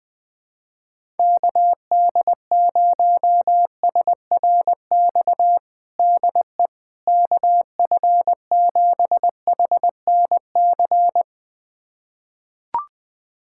Morse code, heard as a keyed tone, about 20 wpm, 700 Hz. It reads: KD0SRX DE KF7HNC
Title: USA Callsign DE Callsign - 6 Characters - 20wpm